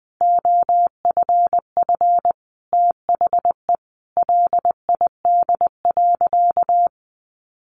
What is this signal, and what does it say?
Morse code, heard as a keyed tone, about 20 wpm, 700 Hz. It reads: OFF THE LID.
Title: Easy Phrases 20wpm